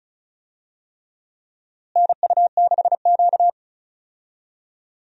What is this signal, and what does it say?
Morse code, heard as a keyed tone, about 34 wpm, 700 Hz. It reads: NU6Q
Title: USA Call Signs - 4 Characters - Repeated 3x - 35wpm